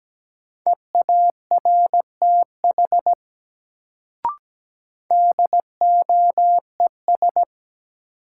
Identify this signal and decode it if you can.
Morse code, heard as a keyed tone, about 17 wpm, 700 Hz. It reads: EARTH DOES
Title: Sentences from Top 200 Words - New Word Drill - 17wpm